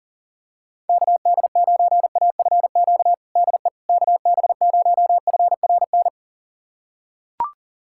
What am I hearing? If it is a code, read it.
Morse code, heard as a keyed tone, about 40 wpm, 700 Hz. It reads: KD9AFQ DE KB0FRN
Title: USA Callsign DE Callsign - 6 Characters - 40wpm